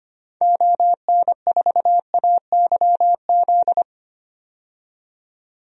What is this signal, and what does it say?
Morse code, heard as a keyed tone, about 25 wpm, 700 Hz. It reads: ON4AYZ
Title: DXCC-100 Call Signs - 6 Characters - 25wpm